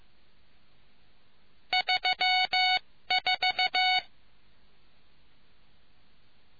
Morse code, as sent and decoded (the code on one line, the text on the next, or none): ...-- ....-
34